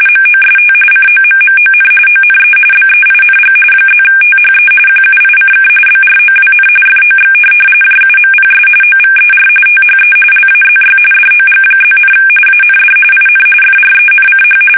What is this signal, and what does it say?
RTTY, about 45.45 baud. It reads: MOST COMMON ONE BEING +THE QUICK BROWN FOX JUMPS OVER THE LAZY DOG+, AND IN FRENCH
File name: fsk850 / teletype-RTTY